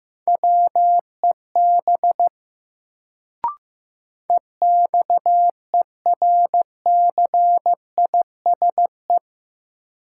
Morse code, heard as a keyed tone, about 15 wpm, 700 Hz. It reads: WEB EXERCISE
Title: Sentences from Top 1000 Words - Review Part 4 - New Word Drill - 15wpm